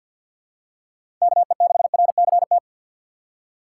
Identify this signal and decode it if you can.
Morse code, heard as a keyed tone, about 53 wpm, 700 Hz. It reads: KE6RCT
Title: USA Call Signs - 6 Characters - 4x Character Spacing - 50wpm